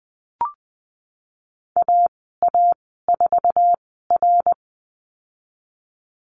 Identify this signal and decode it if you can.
Morse code, heard as a keyed tone, about 20 wpm, spaced as for 15 wpm, 700 Hz. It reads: AA4R
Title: 055 Lesson 10g - Letter r - Call Signs 20wpm